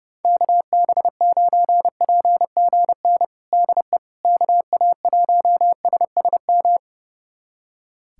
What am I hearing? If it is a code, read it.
Morse code, heard as a keyed tone, about 30 wpm, 700 Hz. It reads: KB9PGN DE KA1SSM
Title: USA Callsign DE Callsign - 6 Characters - 30wpm